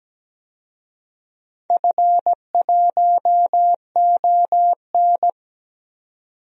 Morse code, heard as a keyed tone, about 17 wpm, 700 Hz. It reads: F1ON
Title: DXCC-100 Call Signs - 4 Characters - 17wpm